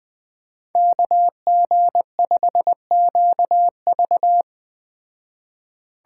Morse code, heard as a keyed tone, about 20 wpm, 700 Hz. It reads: KG5QV
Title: Speed Racing - USA 2x2 Call Signs 17wpm